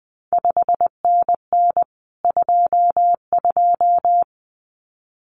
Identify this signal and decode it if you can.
Morse code, heard as a keyed tone, about 20 wpm, 700 Hz. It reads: 5NN 22
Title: CQ WW DX Contest Exchange Encore - 20wpm